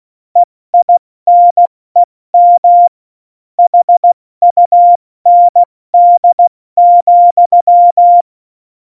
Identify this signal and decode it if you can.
Morse code, heard as a keyed tone, about 16 wpm, 700 Hz. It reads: EINEM HUND,
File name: Morsekurs / Witze / Witz 06. Oktober 2020, 70 F80